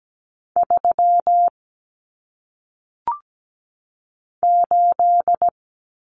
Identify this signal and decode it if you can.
Morse code, heard as a keyed tone, about 17 wpm, 700 Hz. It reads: 3 8
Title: Common Cut Numbers - 1-Digit 17wpm